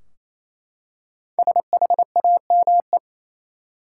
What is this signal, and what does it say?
Morse code, heard as a keyed tone, about 28 wpm, 700 Hz. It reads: SHAME
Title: Five Letter Words - 28wpm